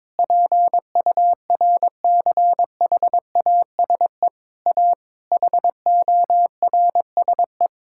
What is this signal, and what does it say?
Morse code, heard as a keyed tone, about 22 wpm, 700 Hz. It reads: PURCHASE A HORSE
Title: Sentences from Top 1000 Words - Review Part 5 - 22wpm